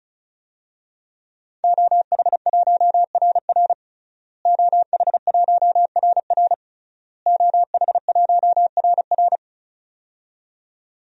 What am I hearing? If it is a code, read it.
Morse code, heard as a keyed tone, about 36 wpm, 700 Hz. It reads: OH1RR OH1RR OH1RR
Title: DXCC-100 Call Signs - 5 Characters - Repeated 3x - 35wpm